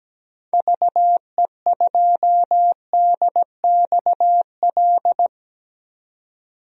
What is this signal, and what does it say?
Morse code, heard as a keyed tone, about 17 wpm, 700 Hz. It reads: VE2DXL
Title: DXCC-100 Call Signs - 6 Characters - 17wpm